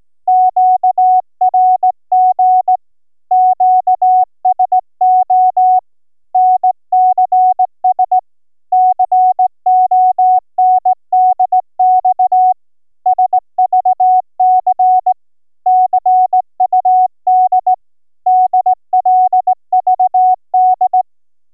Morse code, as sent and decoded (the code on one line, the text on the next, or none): --.- .-. --. / --.- ... --- / -. -.-. ... / -.-. --- -. -.. -..- / ... ...- -.-. / -.-. ..- -.. / -.. .-.. ...- -..
QRG QSO NCS CONDX SVC CUD DLVD